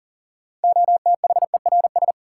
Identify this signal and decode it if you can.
Morse code, heard as a keyed tone, about 40 wpm, 700 Hz. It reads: OTHERS
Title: Sentences from Top 1000 Words - Review Part 2 - New Word Drill - 40wpm